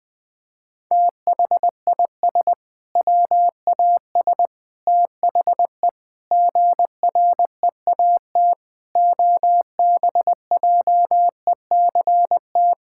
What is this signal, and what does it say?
Morse code, heard as a keyed tone, about 19 wpm, 700 Hz. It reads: THIS WAS THE GREAT OBJECT
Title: Sentences from Top 1000 Words - Review Part 5 - 20wpm